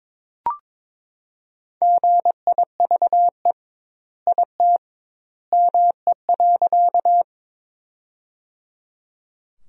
Morse code, heard as a keyed tone, about 22 wpm, 700 Hz. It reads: GIVE IT ME.
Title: 2x Word-Spacing - Sentences from Top 200 Words - 22wpm